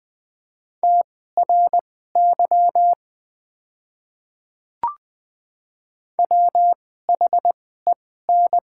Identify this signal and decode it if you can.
Morse code, heard as a keyed tone, about 20 wpm, spaced as for 14 wpm, 690 Hz. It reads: TRY WHEN
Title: 2x Character Spacing - Top 500 Words - 20wpm